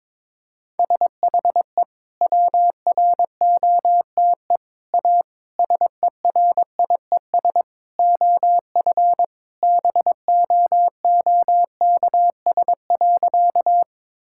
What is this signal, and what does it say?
Morse code, heard as a keyed tone, about 22 wpm, 700 Hz. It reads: SHE WROTE A SERIES OF BOOKS.
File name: practice / 22wpm / Sentences with 3-Word Phrases - 22wpm